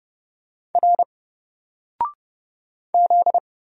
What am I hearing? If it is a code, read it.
Morse code, heard as a keyed tone, about 30 wpm, 700 Hz. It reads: R Z
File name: Single-Letter-Number-30wpm